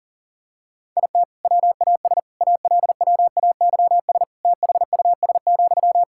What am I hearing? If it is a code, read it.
Morse code, heard as a keyed tone, about 40 wpm, 700 Hz. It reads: IT WAS ALWAYS THUS,
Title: Sentences from Top 1000 Words - Review Part 5 - 40wpm